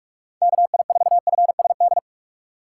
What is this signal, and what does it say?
Morse code, heard as a keyed tone, about 46 wpm, 700 Hz. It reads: KI4FSD
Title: USA Call Signs - 6 Characters - 4x Character Spacing - 45wpm